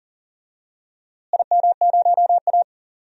Morse code, heard as a keyed tone, about 38 wpm, 700 Hz. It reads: IM0A
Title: DXCC-100 Call Signs - 4 Characters - Repeated 3x - 40wpm